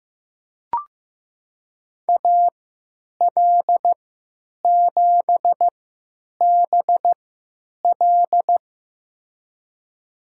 Morse code, heard as a keyed tone, about 15 wpm, spaced as for 9 wpm, 700 Hz. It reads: AL7BL
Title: USA Call Signs - 5 Characters - 3x Character Spacing - 15wpm